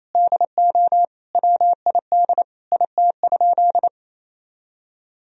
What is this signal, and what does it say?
Morse code, heard as a keyed tone, about 28 wpm, 700 Hz. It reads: DO WID IT?
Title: Sentences from Top 100 Words and Abbreviations 28wpm